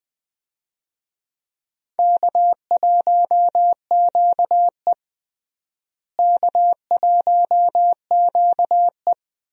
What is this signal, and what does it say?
Morse code, heard as a keyed tone, about 20 wpm, 700 Hz. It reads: K1QE K1QE
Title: USA Call Signs - 4 Characters - Repeated 2x - 20wpm